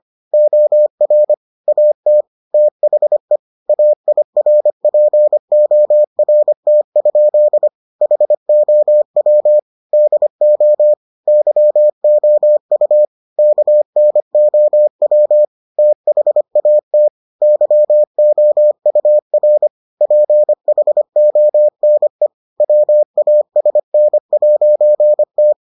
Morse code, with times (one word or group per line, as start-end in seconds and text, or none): 0.3-1.3 OR
1.7-2.2 AT
2.5-3.4 THE
3.7-7.7 AIRPORT?
8.0-9.6 HOW
9.9-10.9 DO
11.3-13.1 YOU
13.4-15.4 KNOW
15.8-17.1 THAT
17.4-19.7 YOUR
20.0-22.3 PHONE
22.6-25.5 WASN'T